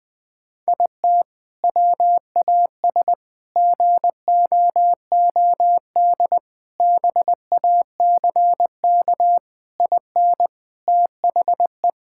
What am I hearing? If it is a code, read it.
Morse code, heard as a keyed tone, about 19 wpm, 700 Hz. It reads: IT WAS GOOD BACK IN THE
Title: Sentences with 3-Word Phrases - Spoken Beginning - 20wpm